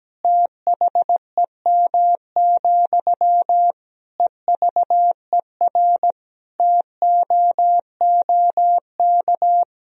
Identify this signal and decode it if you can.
Morse code, heard as a keyed tone, about 17 wpm, 700 Hz. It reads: THEM, EVER TOOK
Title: Sentences from Top 1000 Words - Review Part 1 - 17wpm